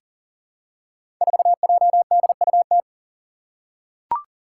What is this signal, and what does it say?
Morse code, heard as a keyed tone, about 39 wpm, 700 Hz. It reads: 4JDUT
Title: Five Challenging Letter-Number Combinations 40wpm